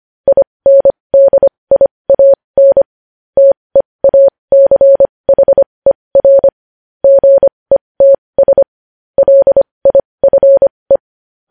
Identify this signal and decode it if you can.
Morse code, heard as a keyed tone, about 25 wpm, spaced as for 20 wpm, 550 Hz. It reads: INDIAN TEACHER GETS LIFE
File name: podcast / News headlines-1x-25x20